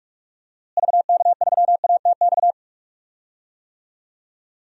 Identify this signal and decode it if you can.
Morse code, heard as a keyed tone, about 43 wpm, 700 Hz. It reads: VK3ATX
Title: DXCC-100 Call Signs - 6 Characters - Repeated 3x - 45wpm